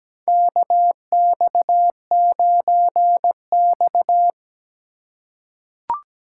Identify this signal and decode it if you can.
Morse code, heard as a keyed tone, about 17 wpm, 700 Hz. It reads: KX9X
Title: USA Call Signs - 4 Characters - 4x Character Spacing - 17wpm